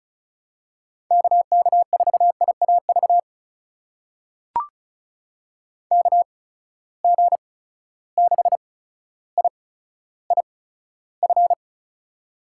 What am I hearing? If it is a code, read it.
Morse code, heard as a keyed tone, about 36 wpm, spaced as for 15 wpm, 700 Hz. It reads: KK4IAV KG6IIF
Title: USA Call Signs - 6 Characters - 8x Character Spacing - 35wpm